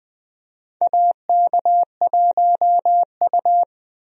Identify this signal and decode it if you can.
Morse code, heard as a keyed tone, about 20 wpm, 700 Hz. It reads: AK1U
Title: USA Call Signs - 4 Characters - Repeated 2x - 20wpm